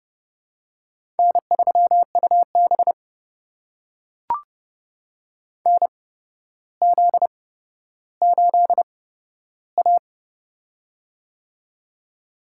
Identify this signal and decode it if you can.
Morse code, heard as a keyed tone, about 32 wpm, spaced as for 13 wpm, 700 Hz. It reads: N3UB NZ8A
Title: USA Call Signs - 4 Characters - 8x Character Spacing - 30wpm